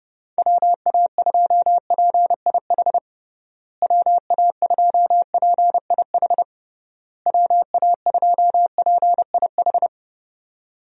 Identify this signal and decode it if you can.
Morse code, heard as a keyed tone, about 31 wpm, 700 Hz. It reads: WA2PIH WA2PIH WA2PIH
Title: USA Call Signs - 6 Characters - Repeated 3x - 30wpm